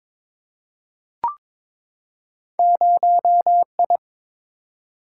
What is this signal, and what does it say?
Morse code, heard as a keyed tone, about 22 wpm, 700 Hz. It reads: 0I
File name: practice / 22wpm / Two Letter-Number Combinations - 22wpm